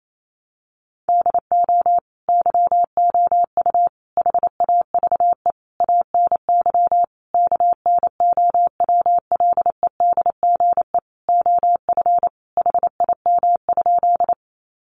Morse code, heard as a keyed tone, about 28 wpm, 700 Hz. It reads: DO YOU HAVE ANY KNOWLEDGE OF HIM?
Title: Sentences from Top 600 Words 28wpm